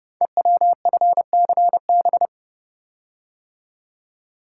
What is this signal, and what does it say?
Morse code, heard as a keyed tone, about 28 wpm, 700 Hz. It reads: EWFCB
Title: Code Groups - 5 Letters - 30wpm